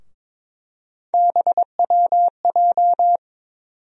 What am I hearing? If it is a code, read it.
Morse code, heard as a keyed tone, about 22 wpm, 700 Hz. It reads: BWJ
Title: USA Call Sign Postfixes - 3 Letters 22wpm